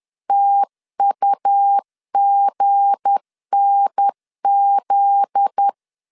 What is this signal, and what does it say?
Morse code, heard as a keyed tone, about 10 wpm, 800 Hz. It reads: TUGNZ
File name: CW / DET / 1 9 1500C S B 10wpm 4439